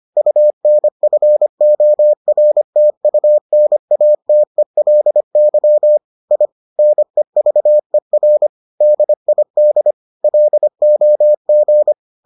Morse code, heard as a keyed tone, about 25 wpm, 600 Hz. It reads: UNFORTUNATELY I NEVER DID LOG